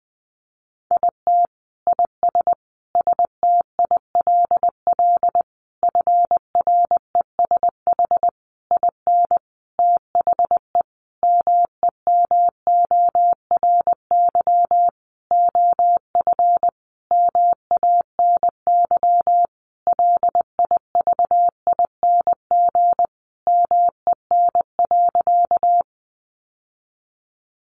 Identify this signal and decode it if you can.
Morse code, heard as a keyed tone, about 20 wpm, 700 Hz. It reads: IT IS STILL FRESH IN THE MEMORY OF MANY LIVING MEN.
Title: Sentences from Top 1300 Words 20wpm